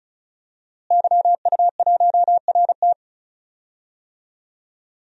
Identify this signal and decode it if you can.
Morse code, heard as a keyed tone, about 34 wpm, 700 Hz. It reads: YU1RT
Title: DXCC-100 Call Signs - 5 Characters - Repeated 3x - 35wpm